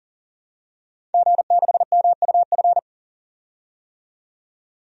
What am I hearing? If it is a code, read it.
Morse code, heard as a keyed tone, about 40 wpm, 700 Hz. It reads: G6MUF
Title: DXCC-100 Call Signs - 5 Characters - 40wpm